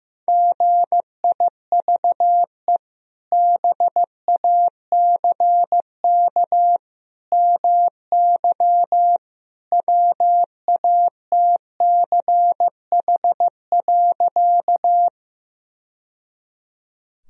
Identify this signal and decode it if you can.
Morse code, heard as a keyed tone, about 15 wpm, 700 Hz. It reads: GIVE BACK MY WATCH.
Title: Sentences from Top 1000 Words - Review Part 1 - 15wpm